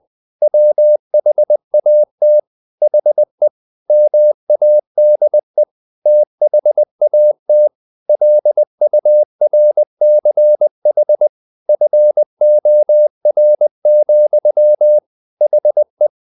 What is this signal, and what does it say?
Morse code, heard as a keyed tone, about 20 wpm, 600 Hz. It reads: WHAT HE MADE THAT LURCH FOR, HE
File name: the old man and the sea / the old man and the sea part 1 20wpm